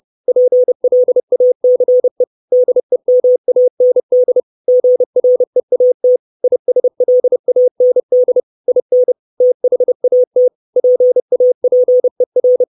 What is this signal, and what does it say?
Morse code, heard as a keyed tone, about 30 wpm, 500 Hz. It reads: PLACE DEMAND GREAT ISLAND IN THAT PAPER
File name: 632 Words 30WPM